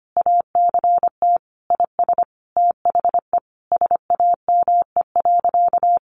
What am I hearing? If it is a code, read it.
Morse code, heard as a keyed tone, about 25 wpm, 700 Hz. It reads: ACT IS THE SAME.
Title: Sentences from Top 1400 Words 25wpm